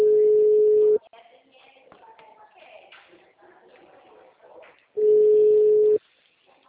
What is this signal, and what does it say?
Ringback tone (Europe, 425 Hz), with speech.
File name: out- -2014- - - .49872